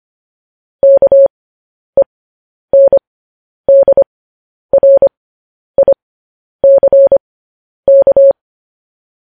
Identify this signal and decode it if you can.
Morse code, heard as a keyed tone, about 25 wpm, spaced as for 10 wpm, 550 Hz. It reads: KENDRICK